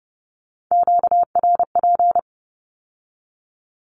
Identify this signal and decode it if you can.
Morse code, heard as a keyed tone, about 30 wpm, 700 Hz. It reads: QRP
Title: Q-Codes with Spoken Meaning - 30wpm